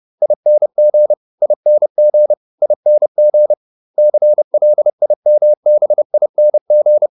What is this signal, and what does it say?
Morse code, heard as a keyed tone, about 30 wpm, 600 Hz. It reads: ING ING ING CLIMBING
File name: ING8 30